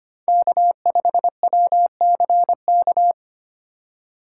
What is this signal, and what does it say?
Morse code, heard as a keyed tone, about 25 wpm, 700 Hz. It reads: K5WCK
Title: USA Call Signs - 2x Character Spacing - 25wpm